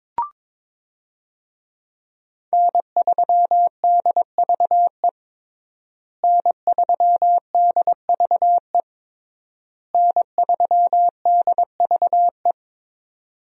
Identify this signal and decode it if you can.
Morse code, heard as a keyed tone, about 22 wpm, 700 Hz. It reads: N3DVE N3DVE N3DVE
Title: USA Call Signs - 5 Characters - Repeated 3x - 22wpm